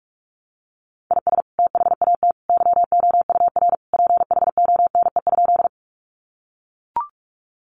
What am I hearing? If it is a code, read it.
Morse code, heard as a keyed tone, about 44 wpm, 700 Hz. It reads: IS THAT YOUR PHONE?
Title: Sentences from Top 1100 Words 45wpm